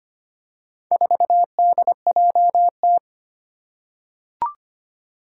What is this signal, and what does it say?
Morse code, heard as a keyed tone, about 25 wpm, 700 Hz. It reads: 4DJT
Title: Four Challenging Letter-Number Combinations 25wpm